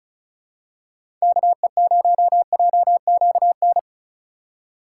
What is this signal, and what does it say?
Morse code, heard as a keyed tone, about 34 wpm, 690 Hz. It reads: KE0JQN
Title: USA Call Signs - 6 Characters - 6x Character Spacing - 35wpm